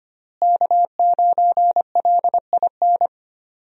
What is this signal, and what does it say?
Morse code, heard as a keyed tone, about 25 wpm, 700 Hz. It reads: K9LIN
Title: USA Call Signs - 5 Characters - Repeated 3x - 25wpm